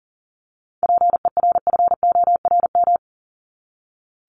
Morse code, heard as a keyed tone, about 40 wpm, 700 Hz. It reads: PERFORM
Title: Top 751-1000 Words 40wpm